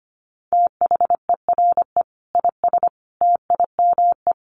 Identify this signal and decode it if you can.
Morse code, heard as a keyed tone, about 25 wpm, 700 Hz. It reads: THERE IS TIME
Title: Sentences from Top 1200 Words 25wpm